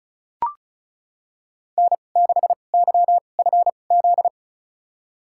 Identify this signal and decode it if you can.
Morse code, heard as a keyed tone, about 36 wpm, spaced as for 25 wpm, 700 Hz. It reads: N6YFZ
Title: USA Call Signs - 2x Character Spacing - 35wpm